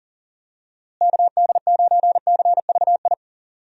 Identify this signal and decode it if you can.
Morse code, heard as a keyed tone, about 40 wpm, 700 Hz. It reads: KD9CVI
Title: USA Call Signs - 6 Characters - Repeated 3x - 40wpm